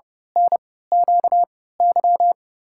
Morse code, heard as a keyed tone, about 30 wpm, spaced as for 17 wpm, 700 Hz. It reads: NQY
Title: USA Call Signs - 3x Character Spacing - 30wpm